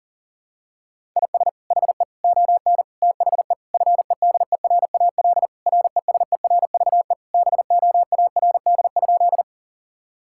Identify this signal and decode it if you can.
Morse code, heard as a keyed tone, about 40 wpm, 700 Hz. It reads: IS HE ON THE FEDERAL RESERVE BOARD?
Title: Sentences from Top 1000 Words - Review Part 6 - 40wpm